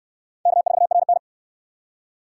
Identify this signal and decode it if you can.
Morse code, heard as a keyed tone, about 69 wpm, 700 Hz. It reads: K4RN